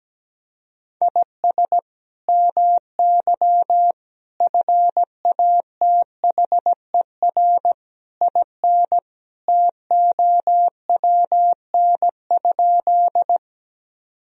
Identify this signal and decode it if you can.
Morse code, heard as a keyed tone, about 17 wpm, 700 Hz. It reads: IS MY FATHER IN TOWN?
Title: Sentences from Top 1000 Words - Review Part 2 - 17wpm